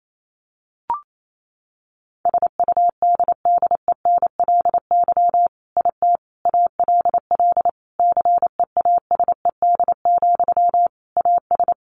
Morse code, heard as a keyed tone, about 28 wpm, 700 Hz. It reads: SUDDENLY IT ALL CEASED, AS
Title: A Sound Of Thunder 28wpm